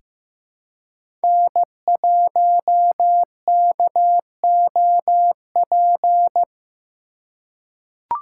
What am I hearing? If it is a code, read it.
Morse code, heard as a keyed tone, about 15 wpm, 700 Hz. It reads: N1KOP
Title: USA Call Signs - 4x Character Spacing - 15wpm